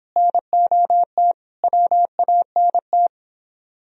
Morse code, heard as a keyed tone, about 29 wpm, 700 Hz. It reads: NOT WANT